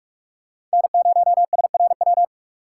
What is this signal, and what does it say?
Morse code, heard as a keyed tone, about 48 wpm, 700 Hz. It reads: N0SRW
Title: USA Call Signs - 5 Characters - 6x Character Spacing - 45wpm